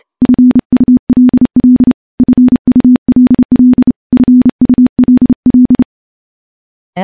Morse code, heard as a keyed tone, about 28 wpm, 250 Hz. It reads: FULL FULL FULL
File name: morse / morse-training-30WPM